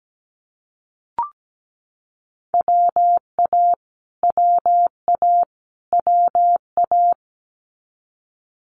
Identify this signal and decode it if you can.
Morse code, heard as a keyed tone, about 17 wpm, 700 Hz. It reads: WA WA WA
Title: Multi-letter Phonemes Repeated 3x 17wpm